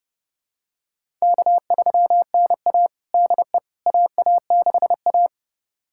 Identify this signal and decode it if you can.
Morse code, heard as a keyed tone, about 31 wpm, 700 Hz. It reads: K3NA DE AA6A
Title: USA Callsign DE Callsign - 4 Characters - 30wpm